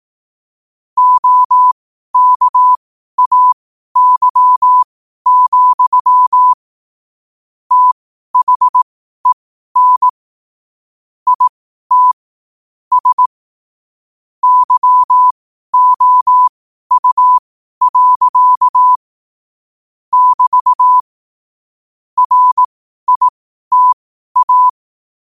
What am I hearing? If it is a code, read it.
Morse code, heard as a keyed tone, about 18 wpm, spaced as for 12 wpm, 1000 Hz. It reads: OKAY, THEN IT S YOU. <BT> RITA
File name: qotdfiles / 12wpm